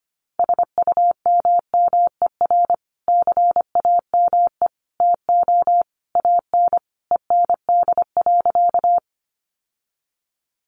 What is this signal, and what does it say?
Morse code, heard as a keyed tone, about 25 wpm, 700 Hz. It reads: SUMMER CAME TO AN END.
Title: Sentences from Top 1100 Words 25wpm